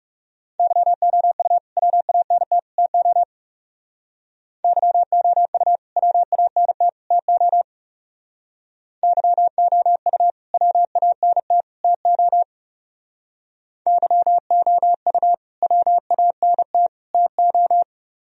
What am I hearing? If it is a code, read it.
Morse code, heard as a keyed tone, about 36 wpm, 700 Hz. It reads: YOU WANT TO YOU WANT TO YOU WANT TO YOU WANT TO
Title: Speed-Racing - Sets of 3 Words 30wpm